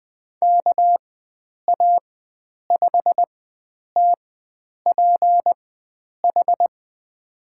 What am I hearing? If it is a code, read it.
Morse code, heard as a keyed tone, about 20 wpm, spaced as for 9 wpm, 700 Hz. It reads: KA5TPH
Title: USA Call Signs - 6 Characters - 4x Character Spacing - 20wpm